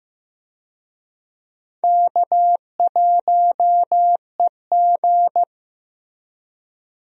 Morse code, heard as a keyed tone, about 15 wpm, 700 Hz. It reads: K1EG
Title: USA Call Signs - 4 Characters - Repeated 3x - 15wpm